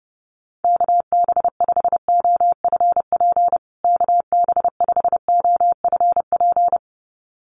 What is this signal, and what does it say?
Morse code, heard as a keyed tone, about 30 wpm, 700 Hz. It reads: KB5OFP KB5OFP
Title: USA Call Signs Repeated 2x 30wpm